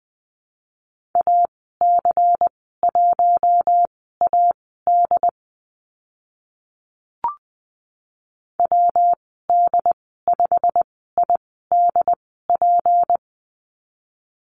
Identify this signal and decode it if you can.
Morse code, heard as a keyed tone, about 20 wpm, spaced as for 14 wpm, 700 Hz. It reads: AC1AD WD5IDP
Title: 167 Lesson 27g - Number 7 - Call Signs 20wpm